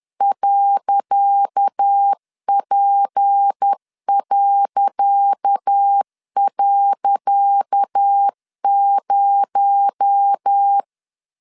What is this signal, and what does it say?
Morse code, heard as a keyed tone, about 10 wpm, 800 Hz. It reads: .P..0
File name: LEZIONI / 10 SIGNS / 10C DOT 10wpm 2940